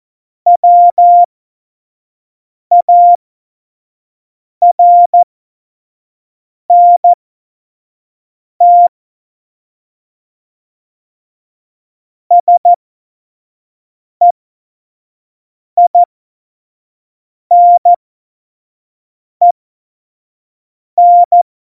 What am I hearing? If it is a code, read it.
Morse code, heard as a keyed tone, about 14 wpm, spaced as for 5 wpm, 700 Hz. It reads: WARNT SEINEN
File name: Morsekurs / Witze / Witz 17 April 2018, 25 F70